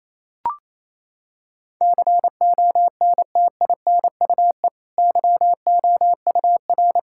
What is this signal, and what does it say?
Morse code, heard as a keyed tone, about 28 wpm, 690 Hz. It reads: CONTINUE YOUR
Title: Sentences from Top 1000 Words - Review Part 6 - 28wpm